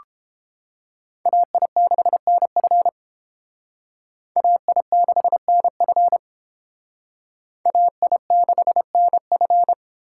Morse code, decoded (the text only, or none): AI6NF AI6NF AI6NF